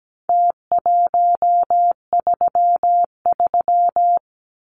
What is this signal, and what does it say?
Morse code, heard as a keyed tone, about 17 wpm, 700 Hz. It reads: T133
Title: Numbers - Contest Style - 4-Digit 17wpm